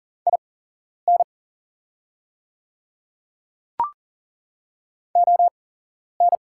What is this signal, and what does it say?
Morse code, heard as a keyed tone, about 38 wpm, spaced as for 11 wpm, 700 Hz. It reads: IN ON
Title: 8x Character Spacing - Top 100 Words - 40wpm